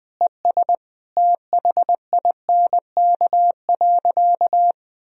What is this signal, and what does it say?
Morse code, heard as a keyed tone, about 21 wpm, 700 Hz. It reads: ES THINK.
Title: Sentences from Top 500 Words and Abbreviations 20wpm